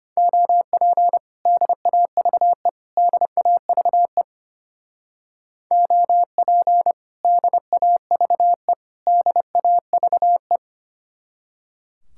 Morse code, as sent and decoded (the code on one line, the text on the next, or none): --- .--. / -.. .- ...- . / -.. .- ...- . / --- .--. / -.. .- ...- . / -.. .- ...- .
OP DAVE DAVE OP DAVE DAVE